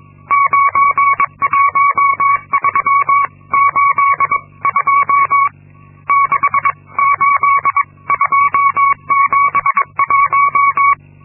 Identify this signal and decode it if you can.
Morse code, heard as a keyed tone, about 22 wpm, spaced as for 19 wpm, 1120 Hz. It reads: 91382 68271
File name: Linksmakalnis-2